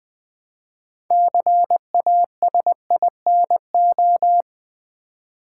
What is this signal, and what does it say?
Morse code, heard as a keyed tone, about 20 wpm, 700 Hz. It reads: CASINO
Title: Build A Word - 3 Letters At A Time - 6-Letter Words - 20wpm